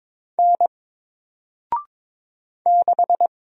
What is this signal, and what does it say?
Morse code, heard as a keyed tone, about 22 wpm, 700 Hz. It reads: N 6
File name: Single-Letter-Number-22wpm